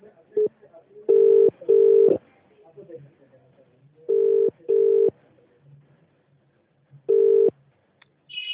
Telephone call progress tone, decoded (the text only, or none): ringback tone